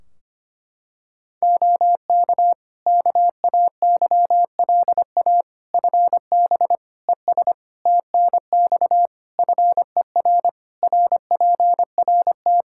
Morse code, heard as a keyed tone, about 25 wpm, 700 Hz. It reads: OK KAYLA FB ES TNX FER RPRT
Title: K1USN SST QSO Practice 25wpm